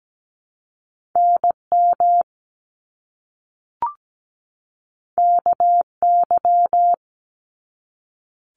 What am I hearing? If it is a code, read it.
Morse code, heard as a keyed tone, about 17 wpm, 700 Hz. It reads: NM KY
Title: US States 17wpm